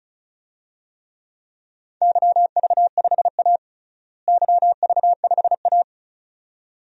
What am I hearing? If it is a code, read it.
Morse code, heard as a keyed tone, about 34 wpm, 700 Hz. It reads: YV5A YV5A
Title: DXCC-100 Call Signs - 4 Characters - Repeated 2x - 35wpm